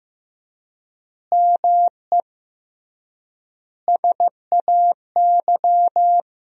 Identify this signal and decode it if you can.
Morse code, heard as a keyed tone, about 15 wpm, 700 Hz. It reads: ME SAY
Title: 3x Word-Spacing - Sentences from Top 500 Words - 15wpm